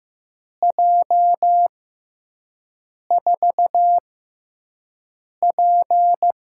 Morse code, heard as a keyed tone, about 15 wpm, spaced as for 5 wpm, 700 Hz. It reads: J4P
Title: USA Call Signs - 4 Characters - 6x Character Spacing - 15wpm